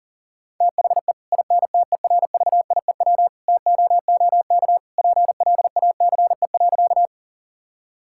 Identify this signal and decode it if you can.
Morse code, heard as a keyed tone, about 40 wpm, 700 Hz. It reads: THE INTERVIEW TOOK PLACE.
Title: Sentences from Top 1600 Words 40wpm